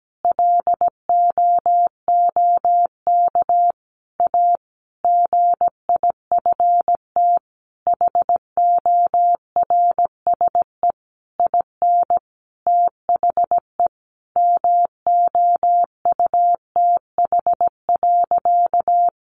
Morse code, heard as a keyed tone, about 17 wpm, 700 Hz. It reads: LOOK A GIFT HORSE IN THE MOUTH.
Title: Sentences from Top 1500 Words 17wpm